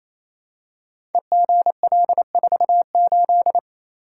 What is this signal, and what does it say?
Morse code, heard as a keyed tone, about 27 wpm, 700 Hz. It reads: EGL48
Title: Code Groups - 5 Letter:Numbers - 28wpm